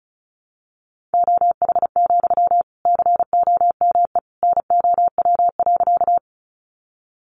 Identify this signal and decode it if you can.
Morse code, heard as a keyed tone, about 34 wpm, 700 Hz. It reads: OH, COME NOW.
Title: Sentences from Top 500 Words 35wpm